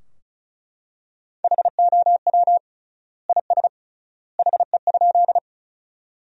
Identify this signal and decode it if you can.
Morse code, heard as a keyed tone, about 36 wpm, 700 Hz. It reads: HOW IS HE?
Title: 3x Word-Spacing - Sentences from Top 100 Words - 35wpm